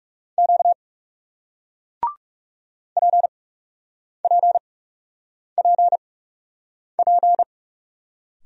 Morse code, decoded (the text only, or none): Q PPPP